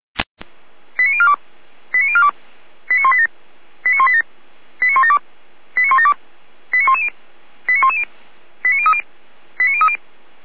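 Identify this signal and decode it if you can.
Five-tone selective call (ZVEI), heard as quick runs of five tones, tones of about 70 ms. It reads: 89042 89042 89187 89187 89172 89172 89110 89110 89030 89030
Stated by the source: ZVEI five-tone selective call received by SDR, fire brigade Feuerwehr Orlamünde.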